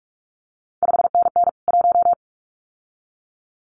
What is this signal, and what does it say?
Morse code, heard as a keyed tone, about 46 wpm, 700 Hz. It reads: 5NN 1
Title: CQ WW DX Contest Exchange - 45wpm